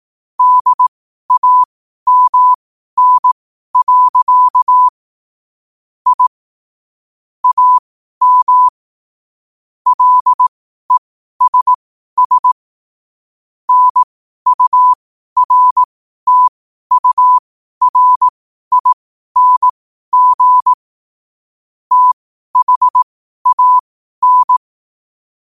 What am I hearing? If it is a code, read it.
Morse code, heard as a keyed tone, about 18 wpm, spaced as for 12 wpm, 1000 Hz. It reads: DAMN. I AM LESS NURTURING THAN
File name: qotdfiles / 12wpm